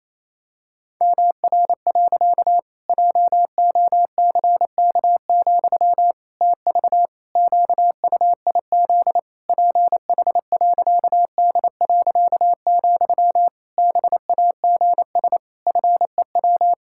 Morse code, heard as a keyed tone, about 28 wpm, 700 Hz. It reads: MR. JOCK, TV QUIZ PH.D., BAGS FEW
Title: Pangram Challenge v2 - 28wpm